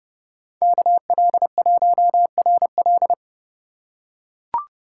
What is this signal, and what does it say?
Morse code, heard as a keyed tone, about 30 wpm, 700 Hz. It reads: KL1RL